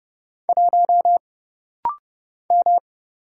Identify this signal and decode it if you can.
Morse code, heard as a keyed tone, about 30 wpm, 700 Hz. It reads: 1 M
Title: Single Letter-Number Alt - Rapid-Fire 30wpm